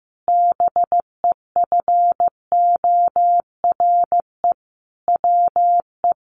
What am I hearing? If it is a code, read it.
Morse code, heard as a keyed tone, about 14 wpm, 700 Hz. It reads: BEFORE WE
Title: Sentences from Top 1500 Words 15wpm